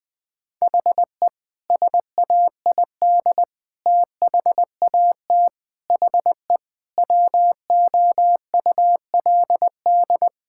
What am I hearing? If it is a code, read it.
Morse code, heard as a keyed tone, about 20 wpm, 700 Hz. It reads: HE SAID THAT HE WOULD
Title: Sentences from Top 1000 Words - Review Part 6 - 20wpm